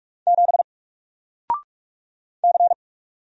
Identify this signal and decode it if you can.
Morse code, heard as a keyed tone, about 44 wpm, 700 Hz. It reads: 7 C
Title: Single Letter-Number 45wpm